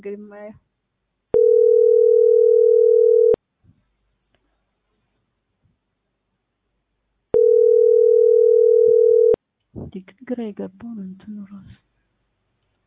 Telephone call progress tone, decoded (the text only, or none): ringback tone